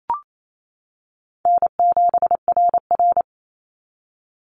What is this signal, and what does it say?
Morse code, heard as a keyed tone, about 28 wpm, 700 Hz. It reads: N7RR
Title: NA SOTA Activator Callsigns 28wpm